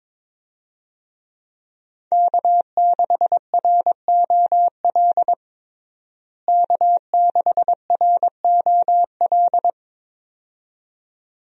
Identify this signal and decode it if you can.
Morse code, heard as a keyed tone, about 22 wpm, 700 Hz. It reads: K6ROL K6ROL
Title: USA Call Signs - 5 Characters - Repeated 2x - 22wpm